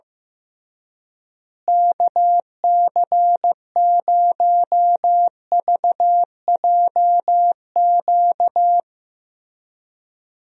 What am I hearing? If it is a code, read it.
Morse code, heard as a keyed tone, about 15 wpm, 700 Hz. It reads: KC0VJQ
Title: USA Call Signs - 6 Characters - Repeated 3x - 15wpm